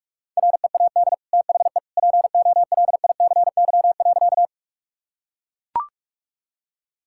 Morse code, heard as a keyed tone, about 41 wpm, 700 Hz. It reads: READ THE POLICY.
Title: Sentences from Top 1000 Words - Review Part 5 - 45wpm